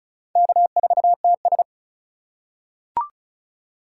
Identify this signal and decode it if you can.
Morse code, heard as a keyed tone, about 34 wpm, 700 Hz. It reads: K4TS
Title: USA Call Signs - 4 Characters - 4x Character Spacing - 35wpm